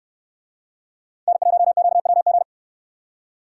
Morse code, heard as a keyed tone, about 66 wpm, 700 Hz. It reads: N1QWG